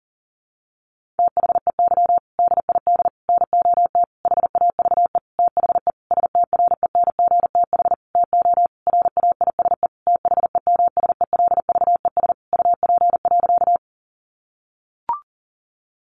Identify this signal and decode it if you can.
Morse code, heard as a keyed tone, about 40 wpm, 700 Hz. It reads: THEY DID NOT HAVE THE STRENGTH TO RAISE THEMSELVES UP.
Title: Sentences from Top 1400 Words 40wpm